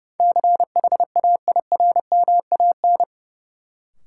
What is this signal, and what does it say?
Morse code, heard as a keyed tone, about 30 wpm, 700 Hz. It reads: CHAIRMAN
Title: Sets of 3 Words - New Word Drill - 30wpm